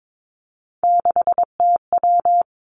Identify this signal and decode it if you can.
Morse code, heard as a keyed tone, about 22 wpm, 700 Hz. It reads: 6TW
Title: Three Challenging Letter-Number Combinations 22wpm